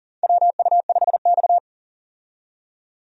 Morse code, heard as a keyed tone, about 39 wpm, 700 Hz. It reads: WU5X
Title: USA Call Signs - 4 Characters - 40wpm